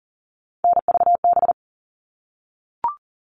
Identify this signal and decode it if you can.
Morse code, heard as a keyed tone, about 40 wpm, 700 Hz. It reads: NVB
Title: All Cut Numbers - 3-Digit 40wpm